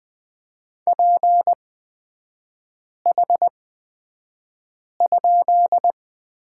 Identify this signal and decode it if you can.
Morse code, heard as a keyed tone, about 20 wpm, spaced as for 5 wpm, 700 Hz. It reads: PH?
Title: 250 Lesson 40e - BK - Three Characters 20wpm